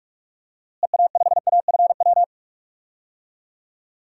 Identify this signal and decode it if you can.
Morse code, heard as a keyed tone, about 46 wpm, 700 Hz. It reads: EA5AFW
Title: DXCC-100 Call Signs - 6 Characters - Repeated 2x - 45wpm